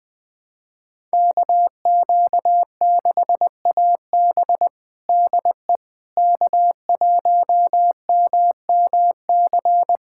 Morse code, heard as a keyed tone, about 20 wpm, 700 Hz. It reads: KQ6AB DE K1MMC
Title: USA Callsign DE Callsign - 5 Characters - 20wpm